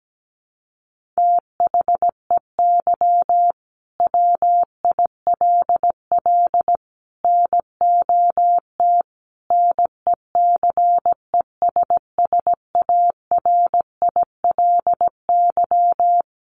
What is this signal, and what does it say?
Morse code, heard as a keyed tone, about 17 wpm, 700 Hz. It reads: THEY WILL NOT NECESSARILY
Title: Sentences from Top 1400 Words 17wpm